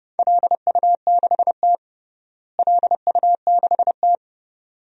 Morse code, heard as a keyed tone, about 31 wpm, 700 Hz. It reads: LU6T LU6T
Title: DXCC-100 Call Signs - 4 Characters - Repeated 2x - 30wpm